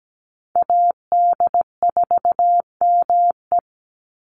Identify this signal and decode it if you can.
Morse code, heard as a keyed tone, about 17 wpm, 700 Hz. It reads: AD4ME
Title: USA Call Signs 17wpm